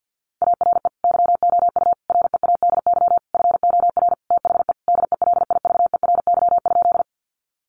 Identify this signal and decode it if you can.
Morse code, heard as a keyed tone, about 49 wpm, 700 Hz. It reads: ARE YOU READY FOR THE DELIVERY?
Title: Sentences from Top 1500 Words 50wpm